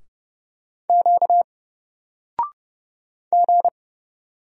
Morse code, heard as a keyed tone, about 30 wpm, 700 Hz. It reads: Q G
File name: Single-Letter-Number-30wpm